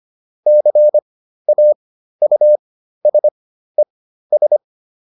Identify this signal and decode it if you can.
Morse code, heard as a keyed tone, about 25 wpm, spaced as for 13 wpm, 600 Hz. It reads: CAUSES